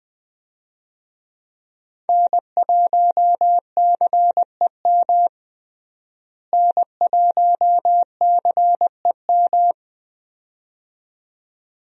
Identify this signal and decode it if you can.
Morse code, heard as a keyed tone, about 20 wpm, 700 Hz. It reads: N1CEM N1CEM
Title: USA Call Signs - 5 Characters - Repeated 2x - 20wpm